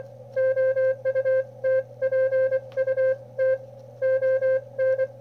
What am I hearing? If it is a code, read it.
Morse code, heard as a keyed tone, about 24 wpm, spaced as for 21 wpm, 520 Hz. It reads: OUTPUT ON
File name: qrp / wb0jnr at 400mW with ae5x 02-23-2018